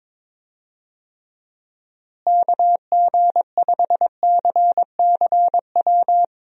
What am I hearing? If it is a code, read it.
Morse code, heard as a keyed tone, about 22 wpm, 700 Hz. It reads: KG5CCW
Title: USA Call Signs - 6 Characters - Repeated 3x - 22wpm